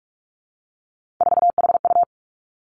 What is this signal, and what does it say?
Morse code, heard as a keyed tone, about 46 wpm, 700 Hz. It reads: 4HU
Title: Three Challenging Letter-Number Combinations 45wpm